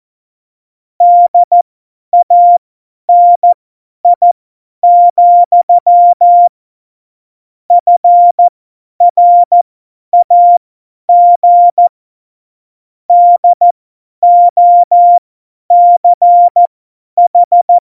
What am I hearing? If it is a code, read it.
Morse code, heard as a keyed tone, about 14 wpm, spaced as for 10 wpm, 700 Hz. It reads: DANI, FRAG DOCH
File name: Morsekurs / Witze / Witz 10. Oktober 17, Tempo 50